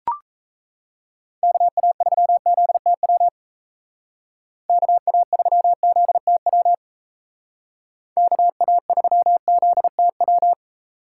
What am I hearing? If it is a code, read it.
Morse code, heard as a keyed tone, about 34 wpm, 700 Hz. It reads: KA3ZTW KA3ZTW KA3ZTW
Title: Speed Racing - USA 2x3 Call Signs 28wpm